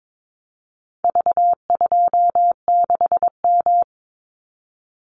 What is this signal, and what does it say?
Morse code, heard as a keyed tone, about 22 wpm, 700 Hz. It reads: V26M
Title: Call Signs Repeated 2x - 22wpm